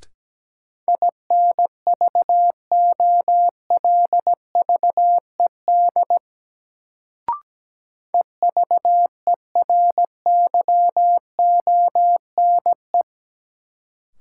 Morse code, heard as a keyed tone, about 17 wpm, 700 Hz. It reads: INVOLVED EVERYONE
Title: Sentences from Top 800 Words - New Word Drill - 17wpm